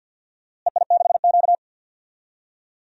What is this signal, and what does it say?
Morse code, heard as a keyed tone, about 52 wpm, 700 Hz. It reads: EI6Q
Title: DXCC-100 Call Signs - 4 Characters - Repeated 3x - 50wpm